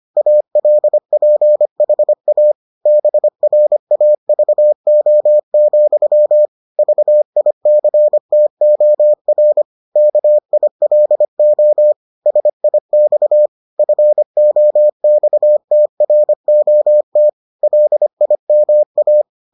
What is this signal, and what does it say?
Morse code, heard as a keyed tone, about 25 wpm, 600 Hz. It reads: ALPHA BRAVO, VICTOR KILO SIX FOXTROT LIMA